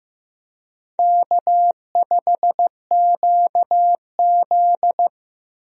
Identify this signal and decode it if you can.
Morse code, heard as a keyed tone, about 15 wpm, 700 Hz. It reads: K5QZ
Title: USA Call Signs - 4 Characters - Repeated 2x - 15wpm